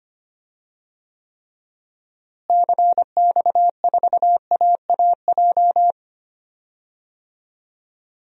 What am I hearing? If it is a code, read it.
Morse code, heard as a keyed tone, about 25 wpm, 700 Hz. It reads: CX4AAJ
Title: DXCC-100 Call Signs - 6 Characters - Repeated 3x - 25wpm